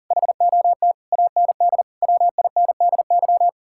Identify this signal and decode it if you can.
Morse code, heard as a keyed tone, about 48 wpm, 700 Hz. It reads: HOT AND WINDY
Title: Speed-Racing Common QSO Phrases Encore - 40wpm